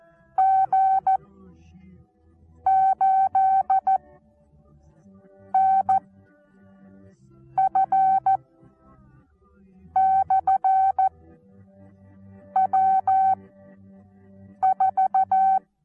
Morse code, heard as a keyed tone, about 14 wpm, spaced as for 5 wpm, 770 Hz. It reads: G8NF/W4
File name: Onlinelehrgang / morsen / DJ4UF Morsen19